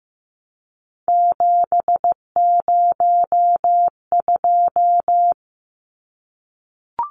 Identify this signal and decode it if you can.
Morse code, heard as a keyed tone, about 15 wpm, 700 Hz. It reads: 702
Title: Numbers - 3-Digit 15wpm